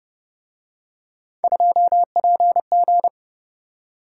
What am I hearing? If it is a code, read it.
Morse code, heard as a keyed tone, about 29 wpm, 700 Hz. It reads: 2PG